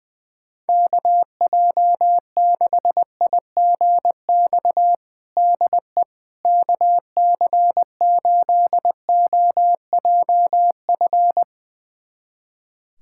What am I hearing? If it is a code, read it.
Morse code, heard as a keyed tone, about 20 wpm, 700 Hz. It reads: KJ6IGX DE KC8OJF
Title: USA Callsign DE Callsign - 6 Characters - 20wpm